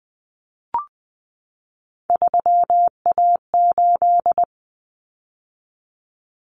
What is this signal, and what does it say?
Morse code, heard as a keyed tone, about 20 wpm, 700 Hz. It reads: 3A8
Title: Common Cut Numbers - 3-Digit 20wpm